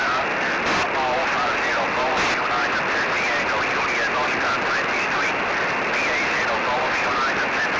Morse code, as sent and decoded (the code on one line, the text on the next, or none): none